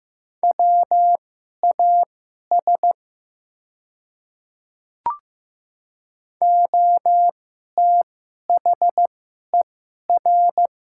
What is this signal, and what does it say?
Morse code, heard as a keyed tone, about 15 wpm, spaced as for 11 wpm, 700 Hz. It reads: WAS OTHER
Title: 2x Character Spacing - Top 100 Words - 15wpm